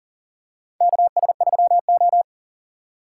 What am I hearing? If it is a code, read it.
Morse code, heard as a keyed tone, about 38 wpm, 700 Hz. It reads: KS3O
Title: USA Call Signs - 4 Characters - 2x Character Spacing - 40wpm